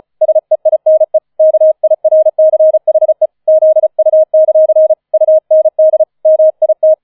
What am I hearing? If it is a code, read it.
Morse code, heard as a keyed tone, about 34 wpm, 630 Hz. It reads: SEINE KIRCHE ZU; UND MIT
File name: 004 Vierter Streich 35wpm